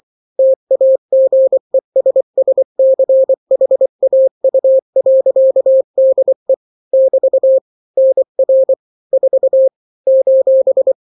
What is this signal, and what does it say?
Morse code, heard as a keyed tone, about 24 wpm, 530 Hz. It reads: TAGESSCHAU.DE <BT> NR 4 :
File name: daily / 23wpm